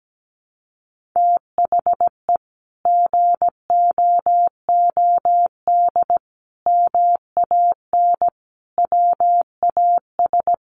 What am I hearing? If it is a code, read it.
Morse code, heard as a keyed tone, about 17 wpm, 700 Hz. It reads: THE GOOD MAN WAS
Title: Sentences from Top 1500 Words 17wpm